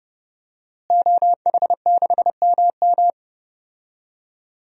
Morse code, heard as a keyed tone, about 30 wpm, 700 Hz. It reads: OH6MM
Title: DXCC-100 Call Signs - 5 Characters - Repeated 3x - 30wpm